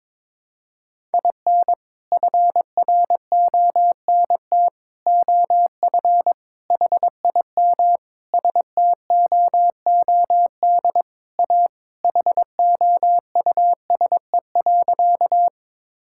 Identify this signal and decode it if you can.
Morse code, heard as a keyed tone, about 22 wpm, 700 Hz. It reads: IN FRONT OF HIM STOOD A HOUSE.
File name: practice / 22wpm / Sentences with 4-Word Phrases - 22wpm